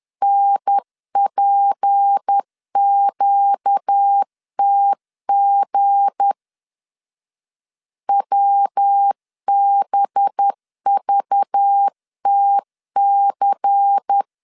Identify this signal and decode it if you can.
Morse code, heard as a keyed tone, about 10 wpm, 800 Hz. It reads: NPQTG WBVTC
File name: LEZIONI / 7 GF9Z / 7 G 10wpm 2501